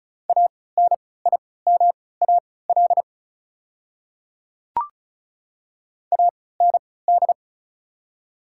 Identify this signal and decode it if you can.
Morse code, heard as a keyed tone, about 36 wpm, spaced as for 20 wpm, 700 Hz. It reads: ANIMAL AND
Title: 3x Character Spacing - Top 200 Words - 35wpm